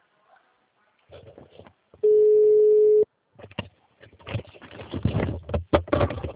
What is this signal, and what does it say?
Ringback tone (Europe, 425 Hz).
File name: out- -2011